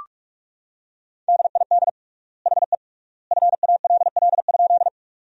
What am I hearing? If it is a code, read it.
Morse code, heard as a keyed tone, about 46 wpm, 700 Hz. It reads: DID HE FALL?
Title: 3x Word-Spacing - Sentences from Top 500 Words - 45wpm